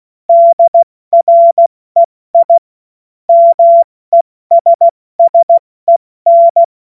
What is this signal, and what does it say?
Morse code, heard as a keyed tone, about 16 wpm, 680 Hz. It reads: DREI MESSEN
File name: Morsekurs / Witze / Witz 09. Dezember 2025 70 F80